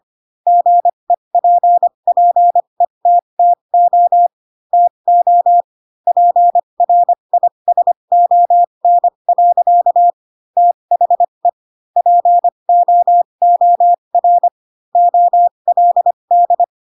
Morse code, heard as a keyed tone, about 25 wpm, spaced as for 22 wpm, 700 Hz. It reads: GEPPETTO TO PRISON. THE POOR OLD
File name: Pinocchio / Pinocchio ch3 18 25 ik1hge